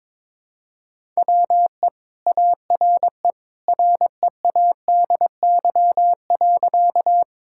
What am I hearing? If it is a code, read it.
Morse code, heard as a keyed tone, about 22 wpm, 700 Hz. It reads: WE ARE READY.
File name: practice / 22wpm / Sentences with 2-Word Phrases - 22wpm